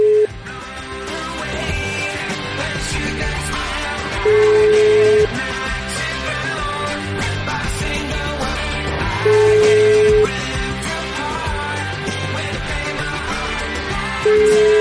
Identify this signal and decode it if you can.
Ringback tone (Europe, 425 Hz), with noise.